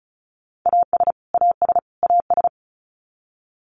Morse code, heard as a keyed tone, about 34 wpm, 700 Hz. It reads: AS AS AS
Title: Top 200 Words Repeated 35wpm